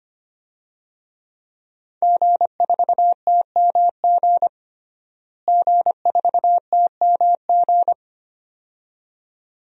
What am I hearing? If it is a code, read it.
Morse code, heard as a keyed tone, about 25 wpm, 700 Hz. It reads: G4TMG G4TMG
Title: DXCC-100 Call Signs - 5 Characters - Repeated 2x - 25wpm